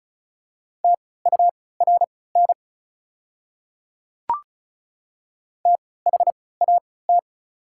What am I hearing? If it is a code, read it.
Morse code, heard as a keyed tone, about 36 wpm, spaced as for 20 wpm, 690 Hz. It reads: TURN THAT
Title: 3x Character Spacing - Top 500 Words - 35wpm